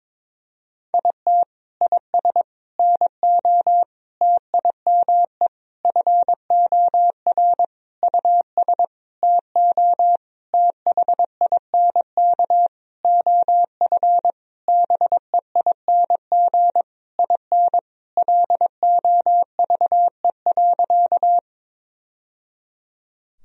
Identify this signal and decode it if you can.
Morse code, heard as a keyed tone, about 22 wpm, 700 Hz. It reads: IT IS NO TIME FOR US TO THINK OF BEING IN LOVE.
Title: Sentences from Top 1000 Words - Review Part 3 - 22wpm